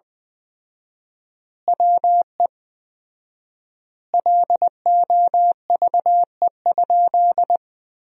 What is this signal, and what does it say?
Morse code, heard as a keyed tone, about 20 wpm, 700 Hz. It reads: WE LOVE?
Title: 4x Word-Spacing - Sentences from Top 400 Words - 20wpm